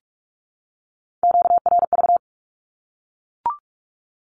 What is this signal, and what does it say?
Morse code, heard as a keyed tone, about 44 wpm, 700 Hz. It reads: QRV
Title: Q-Codes 45wpm